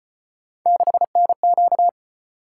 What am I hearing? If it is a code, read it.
Morse code, heard as a keyed tone, about 33 wpm, 700 Hz. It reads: 6NQ